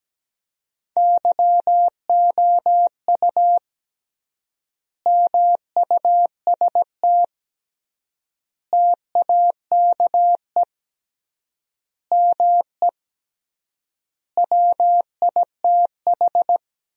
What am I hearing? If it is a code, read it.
Morse code, heard as a keyed tone, about 17 wpm, 700 Hz. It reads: YOU MUST TAKE ME WITH
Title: 3x Word-Spacing - Sentences from Top 200 Words - 17wpm